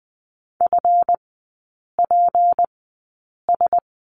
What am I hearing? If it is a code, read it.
Morse code, heard as a keyed tone, about 20 wpm, spaced as for 8 wpm, 700 Hz. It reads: FPS
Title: 152 Lesson 25e - Letter p - Three Characters 20wpm